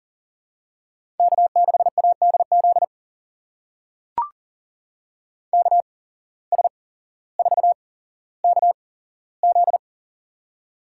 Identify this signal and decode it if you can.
Morse code, heard as a keyed tone, about 38 wpm, spaced as for 17 wpm, 700 Hz. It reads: K6ADZ KS4KZ
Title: USA Call Signs - 8x Character Spacing - 40wpm